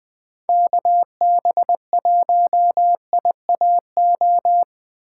Morse code, heard as a keyed tone, about 20 wpm, 700 Hz. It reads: KB1IAO
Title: Speed Racing - USA 2x3 Call Signs 17wpm